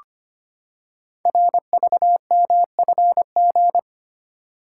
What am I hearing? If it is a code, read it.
Morse code, heard as a keyed tone, about 25 wpm, 700 Hz. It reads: RVMFG
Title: Code Groups - 5 Letter:Numbers - 25wpm